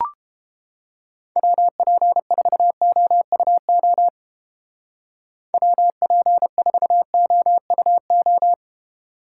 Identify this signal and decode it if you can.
Morse code, heard as a keyed tone, about 30 wpm, 700 Hz. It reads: WP4OUO WP4OUO